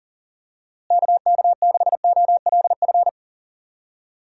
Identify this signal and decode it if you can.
Morse code, heard as a keyed tone, about 40 wpm, 700 Hz. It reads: KK6OLF
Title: USA Call Signs - 6 Characters - Repeated 3x - 40wpm